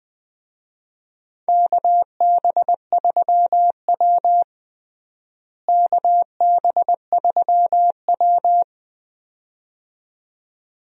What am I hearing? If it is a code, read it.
Morse code, heard as a keyed tone, about 20 wpm, 700 Hz. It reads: KB3W KB3W
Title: USA Call Signs - 4 Characters - Repeated 2x - 20wpm